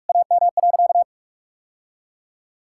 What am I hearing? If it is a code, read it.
Morse code, heard as a keyed tone, about 46 wpm, 700 Hz. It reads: AM.
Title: Sentences from Top 1000 Words - Review Part 6 - 45wpm